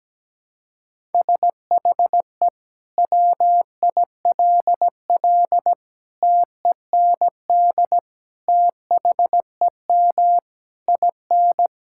Morse code, heard as a keyed tone, about 17 wpm, 700 Hz. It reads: SHE WILL TEND THEM IN
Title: Sentences from Top 1000 Words - Review Part 5 - 17wpm